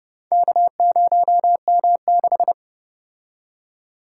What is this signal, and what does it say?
Morse code, heard as a keyed tone, about 29 wpm, 700 Hz. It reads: K0M6